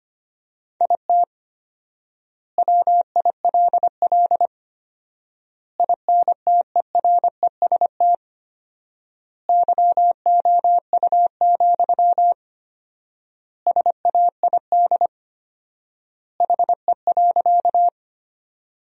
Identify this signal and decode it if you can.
Morse code, heard as a keyed tone, about 25 wpm, 700 Hz. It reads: IT WILL INTEREST YOU, SAID HE.
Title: 4x Word-Spacing - Sentences from Top 400 Words - 25wpm